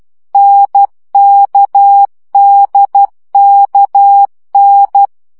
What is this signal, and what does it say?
Morse code, heard as a keyed tone, about 12 wpm, 800 Hz. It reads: NKDKN
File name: CW gyakorlas n d k 10 baud